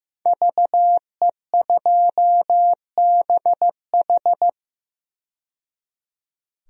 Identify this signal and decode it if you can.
Morse code, heard as a keyed tone, about 15 wpm, 700 Hz. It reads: VE2BH
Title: DXCC-100 Call Signs - 5 Characters - 15wpm